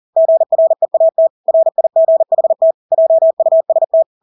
Morse code, heard as a keyed tone, about 40 wpm, 650 Hz. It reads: GREAT RIGHT JUST
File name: Engelske ord 40-40 wpm